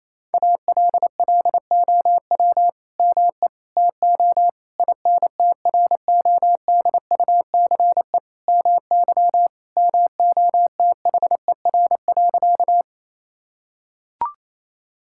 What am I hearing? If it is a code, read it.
Morse code, heard as a keyed tone, about 28 wpm, 700 Hz. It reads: ALLOW ME TO INTRODUCE MY MOTHER.
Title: Sentences from Top 1600 Words 28wpm